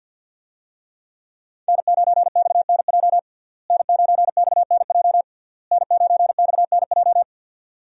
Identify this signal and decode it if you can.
Morse code, heard as a keyed tone, about 51 wpm, 700 Hz. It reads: N9XNJ N9XNJ N9XNJ
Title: USA Call Signs - 5 Characters - Repeated 3x - 50wpm